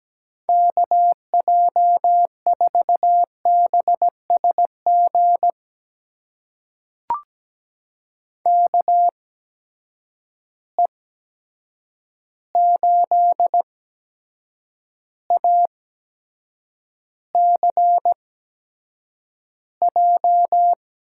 Morse code, heard as a keyed tone, about 17 wpm, spaced as for 7 wpm, 700 Hz. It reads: KJ4BSG KE8ACJ
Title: USA Call Signs - 6 Characters - 8x Character Spacing - 17wpm